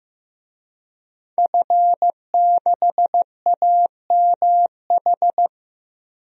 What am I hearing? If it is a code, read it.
Morse code, heard as a keyed tone, about 15 wpm, 700 Hz. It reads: F6AMH
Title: DXCC-100 Call Signs - 5 Characters - Repeated 2x - 15wpm